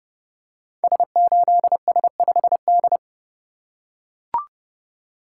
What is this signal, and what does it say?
Morse code, heard as a keyed tone, about 30 wpm, 700 Hz. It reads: S8S5D
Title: Code Groups - 5 Letter:Numbers - 30wpm